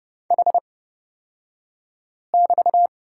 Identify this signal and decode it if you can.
Morse code, heard as a keyed tone, about 30 wpm, spaced as for 5 wpm, 700 Hz. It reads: H<BT>
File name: Lesson 36d - BT - Two Characters 30wpm